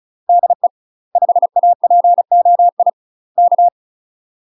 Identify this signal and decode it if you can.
Morse code, heard as a keyed tone, about 34 wpm, 700 Hz. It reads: DE 5APOI K
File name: UK MASTER QSOs TRAINING / 35WPM / UK MASTER QSOs 0042